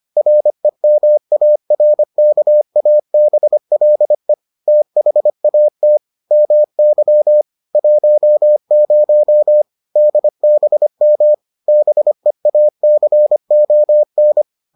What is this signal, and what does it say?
Morse code, heard as a keyed tone, about 25 wpm, 600 Hz. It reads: REMARKABLE THAT MY 10 DBM BEACON